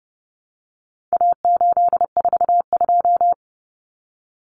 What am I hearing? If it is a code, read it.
Morse code, heard as a keyed tone, about 30 wpm, 700 Hz. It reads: A842
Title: Common Cut Numbers - 4-Digit 30wpm